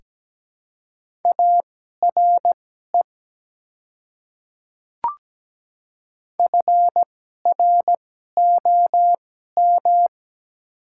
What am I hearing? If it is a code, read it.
Morse code, heard as a keyed tone, about 17 wpm, spaced as for 12 wpm, 700 Hz. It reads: ARE FROM
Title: 2x Character Spacing - Top 100 Words - 17wpm